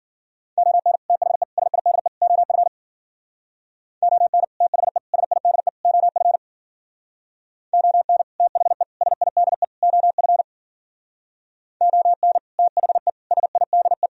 Morse code, heard as a keyed tone, about 52 wpm, 700 Hz. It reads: ON THE SIDE OF ON THE SIDE OF ON THE SIDE OF ON THE SIDE